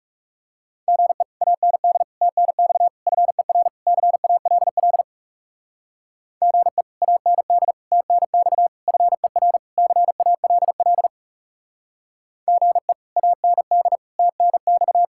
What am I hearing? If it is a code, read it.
Morse code, heard as a keyed tone, about 37 wpm, 700 Hz. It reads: GE AND TNX FER CALL GE AND TNX FER CALL GE AND TNX
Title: Speed-Racing Common QSO Phrases - 30wpm